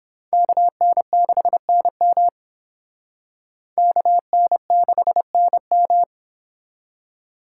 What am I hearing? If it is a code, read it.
Morse code, heard as a keyed tone, about 26 wpm, 700 Hz. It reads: KN6NM KN6NM
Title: Speed Racing - USA 2x2 Call Signs 22wpm